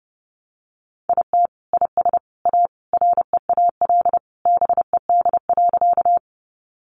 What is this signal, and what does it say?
Morse code, heard as a keyed tone, about 30 wpm, 700 Hz. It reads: IT IS A REAL BED.
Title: Sentences from Top 500 Words 30wpm